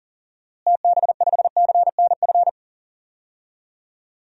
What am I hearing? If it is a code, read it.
Morse code, heard as a keyed tone, about 40 wpm, 700 Hz. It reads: TB5CNF
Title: DXCC-100 Call Signs - 6 Characters - Repeated 2x - 40wpm